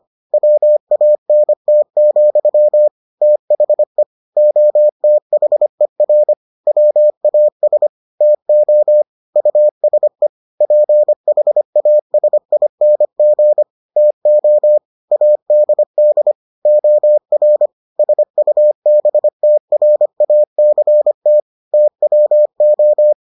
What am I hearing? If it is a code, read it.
Morse code, heard as a keyed tone, about 25 wpm, 600 Hz. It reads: WANT, THE OTHER WAS TO USE PHASING TO ADD OR SUBTRACT TWO